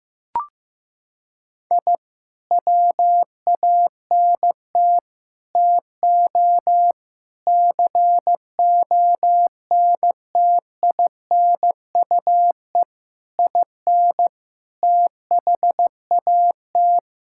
Morse code, heard as a keyed tone, about 15 wpm, 700 Hz. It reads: I WANT TO CONTINUE IN THAT
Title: Sentences from Top 1000 Words - Review Part 6 - 15wpm